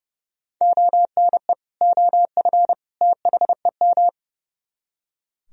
Morse code, heard as a keyed tone, about 38 wpm, 700 Hz. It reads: ONE OF THEM